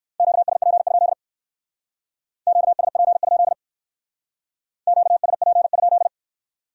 Morse code, heard as a keyed tone, about 55 wpm, 700 Hz. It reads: QSP? QSP? QSP?
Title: Speed-Racing Q-Codes - 45wpm